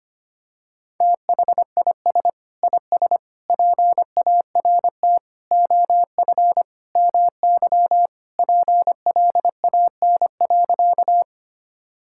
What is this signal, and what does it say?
Morse code, heard as a keyed tone, about 25 wpm, 700 Hz. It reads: THIS IS PART OF MY PLAN.
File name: practice / 25wpm / Sentences with 2-Word Phrases - 25wpm